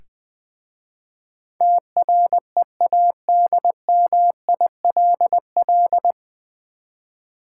Morse code, heard as a keyed tone, about 20 wpm, 700 Hz. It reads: TREADMILL
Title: Easy Words with Hints - 20wpm